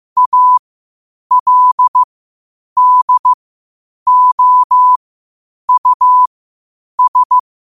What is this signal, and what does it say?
Morse code, heard as a keyed tone, about 15 wpm, spaced as for 8 wpm, 1000 Hz. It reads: ALDOUS
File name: qotdfiles / 8wpm